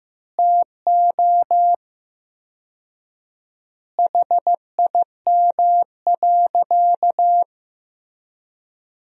Morse code, heard as a keyed tone, about 15 wpm, 700 Hz. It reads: TO HIM.
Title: 4x Word-Spacing - Sentences from Top 400 Words - 15wpm